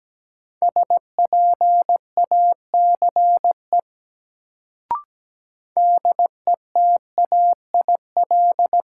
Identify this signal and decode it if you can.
Morse code, heard as a keyed tone, about 17 wpm, 700 Hz. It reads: SPACE DETAIL
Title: Sentences from Top 1500 Words - New Word Drill Alt - 17wpm